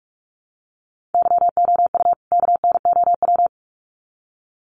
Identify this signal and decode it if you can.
Morse code, heard as a keyed tone, about 44 wpm, 700 Hz. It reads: YOU KNOW
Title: Sets of 2 Words 45wpm